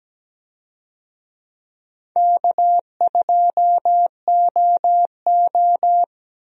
Morse code, heard as a keyed tone, about 17 wpm, 700 Hz. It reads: K2OO
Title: USA Call Signs - 4 Characters - Repeated 2x - 17wpm